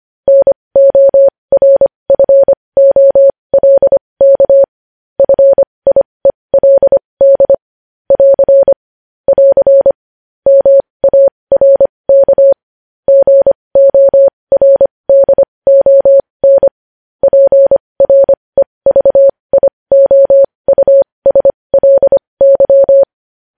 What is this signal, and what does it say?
Morse code, heard as a keyed tone, about 25 wpm, spaced as for 20 wpm, 550 Hz. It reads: NORFOLK FIELD <AR> <AR> MARK GORDON PREVIOUSLY
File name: podcast / News headlines-1x-25x20